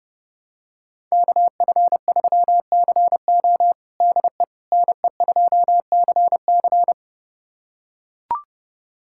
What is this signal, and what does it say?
Morse code, heard as a keyed tone, about 32 wpm, 700 Hz. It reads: KF3CO DE NE2CC
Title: USA Callsign DE Callsign 30wpm